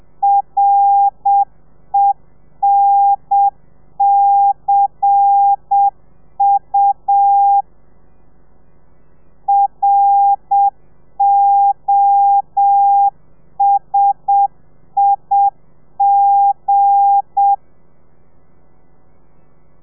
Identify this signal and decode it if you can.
Morse code, heard as a keyed tone, about 7 wpm, 800 Hz. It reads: RENCU ROSIG
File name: telegrafia / i2tfj / Lez8 35